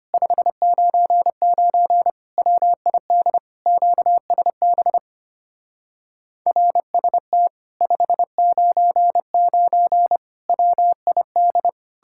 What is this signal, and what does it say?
Morse code, heard as a keyed tone, about 32 wpm, 700 Hz. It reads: 599 WID QSB RST 599 WID